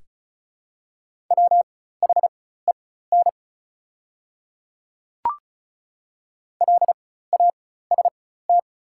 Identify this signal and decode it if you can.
Morse code, heard as a keyed tone, about 36 wpm, spaced as for 16 wpm, 700 Hz. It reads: WHEN LAST
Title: 4x Character Spacing - Top 500 Words - 35wpm